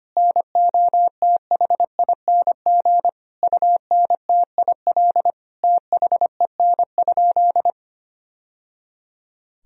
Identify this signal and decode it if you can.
Morse code, heard as a keyed tone, about 25 wpm, 700 Hz. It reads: NOTHING UNTIL THEN?
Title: Sentences from Top 1000 Words - Review Part 1 - 25wpm